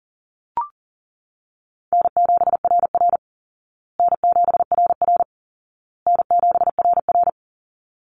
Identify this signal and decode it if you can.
Morse code, heard as a keyed tone, about 37 wpm, 700 Hz. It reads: N7RR N7RR N7RR
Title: Easy Call Signs Repeated 3x - 40wpm